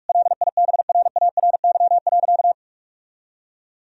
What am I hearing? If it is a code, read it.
Morse code, heard as a keyed tone, about 41 wpm, 700 Hz. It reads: LIBRARY.
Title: Sentences from Top 1000 Words - Review Part 3 - 45wpm